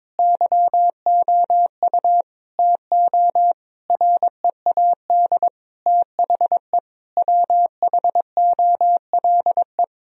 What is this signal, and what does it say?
Morse code, heard as a keyed tone, about 22 wpm, 700 Hz. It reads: YOU TO READ THE WHOLE
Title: Sentences from Top 1000 Words - Review Part 1 - 22wpm